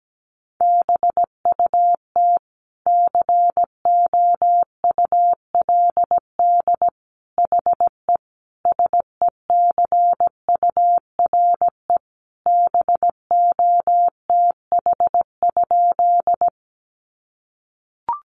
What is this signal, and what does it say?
Morse code, heard as a keyed tone, about 17 wpm, 700 Hz. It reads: BUT COULD HE SECURE BOTH?
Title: Sentences from Top 1200 Words 17wpm